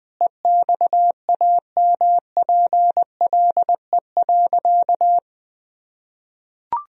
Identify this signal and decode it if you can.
Morse code, heard as a keyed tone, about 20 wpm, 700 Hz. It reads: EXAMPLE.
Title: Sentences from Top 1000 Words - Review Part 1 - 20wpm